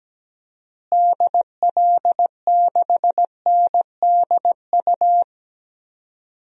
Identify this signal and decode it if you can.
Morse code, heard as a keyed tone, about 17 wpm, 700 Hz. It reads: DL6NDU
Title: DXCC-100 Call Signs - 6 Characters - Repeated 2x - 17wpm